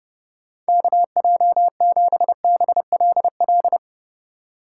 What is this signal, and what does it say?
Morse code, heard as a keyed tone, about 35 wpm, 700 Hz. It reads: KJ7BLL
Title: Speed Racing - USA 2x3 Call Signs 30wpm